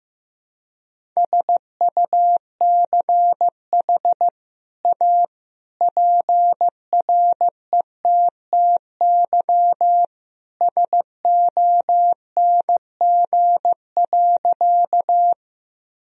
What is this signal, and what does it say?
Morse code, heard as a keyed tone, about 15 wpm, 700 Hz. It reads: SUCH A PRETTY SONG.
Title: Sentences from Top 1000 Words - Review Part 2 - 15wpm